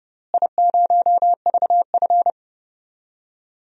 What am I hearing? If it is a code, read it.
Morse code, heard as a keyed tone, about 30 wpm, 700 Hz. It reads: I0VF
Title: DXCC-100 Call Signs - 4 Characters - Repeated 3x - 30wpm